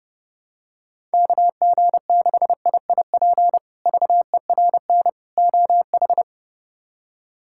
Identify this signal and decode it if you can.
Morse code, heard as a keyed tone, about 30 wpm, 700 Hz. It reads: KG6IIP VERN OH
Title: K1USN SST QSO Practice - 30wpm